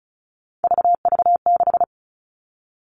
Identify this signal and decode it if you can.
Morse code, heard as a keyed tone, about 34 wpm, 700 Hz. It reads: VV6
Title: All Cut Numbers - 3-Digit 35wpm